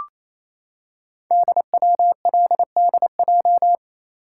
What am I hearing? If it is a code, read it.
Morse code, heard as a keyed tone, about 27 wpm, 700 Hz. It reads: DWLDJ
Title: Code Groups - 5 Letters - 28wpm